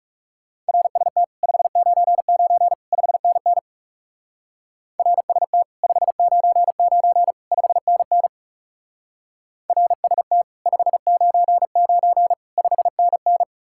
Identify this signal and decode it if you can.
Morse code, heard as a keyed tone, about 35 wpm, 700 Hz. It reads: RST 599 5NN RST 599 5NN RST 599 5NN
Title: Speed-Racing Common QSO Phrases Encore - 30wpm